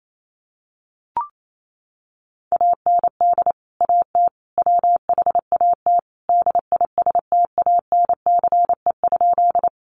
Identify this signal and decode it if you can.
Morse code, heard as a keyed tone, about 28 wpm, 700 Hz. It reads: AND AT WHAT DISTANCE?
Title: Sentences from Top 1200 Words 28wpm